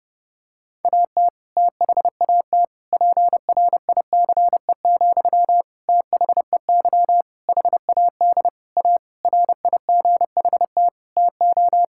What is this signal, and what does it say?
Morse code, heard as a keyed tone, about 30 wpm, 700 Hz. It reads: AT THAT PRICE, THEY HAD A RIGHT TO
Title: Sentences from Top 1000 Words - Review Part 5 - 30wpm